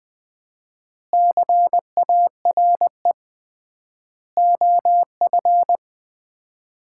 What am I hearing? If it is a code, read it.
Morse code, heard as a keyed tone, about 20 wpm, 700 Hz. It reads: CARE OF
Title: 3x Word-Spacing - Sets of 4 Words - 20wpm